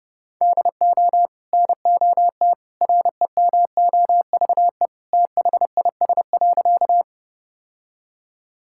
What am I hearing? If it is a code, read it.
Morse code, heard as a keyed tone, about 32 wpm, 700 Hz. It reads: DO NOT REMOVE THIS.
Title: Sentences from Top 1000 Words - Review Part 5 - 30wpm